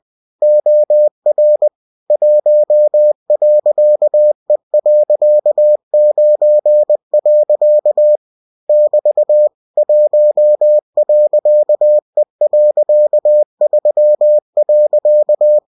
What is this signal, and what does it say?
Morse code, heard as a keyed tone, about 20 wpm, 600 Hz. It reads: OR 1.E.9. <BT> 1.E.3.
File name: winnie-20wpm